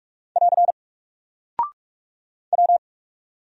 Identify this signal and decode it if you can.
Morse code, heard as a keyed tone, about 44 wpm, 700 Hz. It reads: <AR> W
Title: Single Character (Letter:Number:Punctuation:Prosign) 45wpm